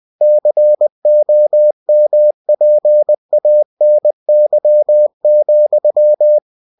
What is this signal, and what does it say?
Morse code, heard as a keyed tone, about 20 wpm, 600 Hz. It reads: COMPANY,
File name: cw-books / winnie-the-pooh / winnie-20wpm-page-0013